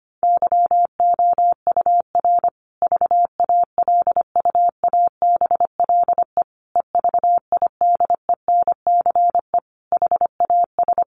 Morse code, heard as a keyed tone, about 25 wpm, 700 Hz. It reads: YOUR VALUABLE EVIDENCE HAS
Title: Sentences from Top 1300 Words 25wpm